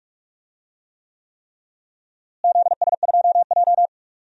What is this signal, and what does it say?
Morse code, heard as a keyed tone, about 45 wpm, 700 Hz. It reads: ZS2J
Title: DXCC-100 Call Signs - 4 Characters - Repeated 3x - 45wpm